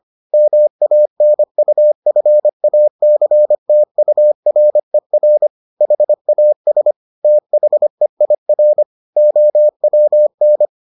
Morse code, heard as a keyed tone, about 25 wpm, 600 Hz. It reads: MANUFACTURER HAS THEIR OWN